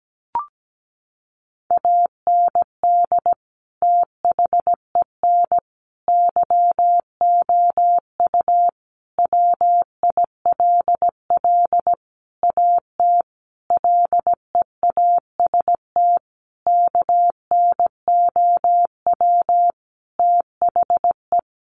Morse code, heard as a keyed tone, about 17 wpm, 700 Hz. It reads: AND THEN YOU WILL AT LEAST KNOW THE
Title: Sentences from Top 1500 Words 17wpm